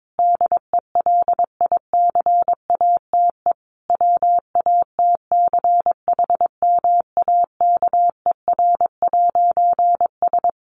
Morse code, heard as a keyed tone, about 22 wpm, 700 Hz. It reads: DELICATE WATCHMAKER'S
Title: A Sound Of Thunder 22wpm